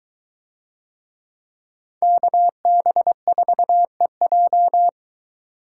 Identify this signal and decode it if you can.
Morse code, heard as a keyed tone, about 22 wpm, 700 Hz. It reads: KB4EJ